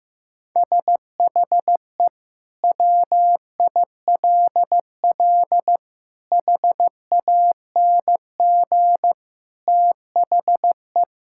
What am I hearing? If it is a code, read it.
Morse code, heard as a keyed tone, about 15 wpm, 700 Hz. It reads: SHE WILL HANG THE
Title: Sentences from Top 1600 Words 15wpm